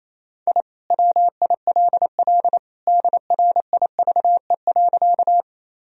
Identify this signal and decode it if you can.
Morse code, heard as a keyed tone, about 28 wpm, 700 Hz. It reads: I WILL DRIVE.
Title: Sentences from Top 1000 Words - Review Part 2 - 28wpm